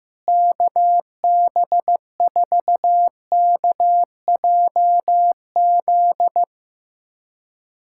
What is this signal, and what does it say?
Morse code, heard as a keyed tone, about 15 wpm, 700 Hz. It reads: KB4KJZ
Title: USA Callsign DE Callsign - 6 Characters - 15wpm